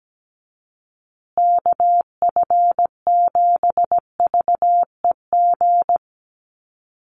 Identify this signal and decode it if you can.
Morse code, heard as a keyed tone, about 17 wpm, 700 Hz. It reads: KF7VEG
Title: USA Call Signs Repeated 2x 17wpm